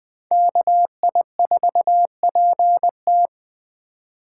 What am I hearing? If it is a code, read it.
Morse code, heard as a keyed tone, about 20 wpm, 700 Hz. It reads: KI4PT
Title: Speed Racing - USA 2x2 Call Signs 17wpm